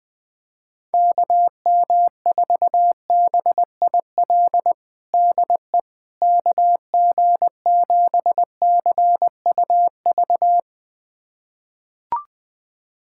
Morse code, heard as a keyed tone, about 20 wpm, 700 Hz. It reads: KM4BIL DE KG7CUV
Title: USA Callsign DE Callsign - 6 Characters - 20wpm